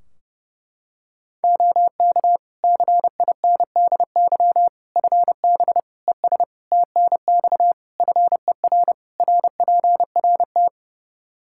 Morse code, heard as a keyed tone, about 30 wpm, 700 Hz. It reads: OK CINDY FB ES TNX FER RPRT
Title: K1USN SST QSO Practice 30wpm